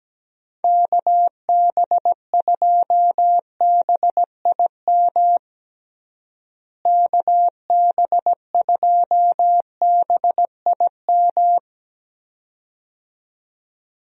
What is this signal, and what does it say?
Morse code, heard as a keyed tone, about 17 wpm, 700 Hz. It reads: KB2BIM KB2BIM
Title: USA Call Signs - 6 Characters - Repeated 2x - 17wpm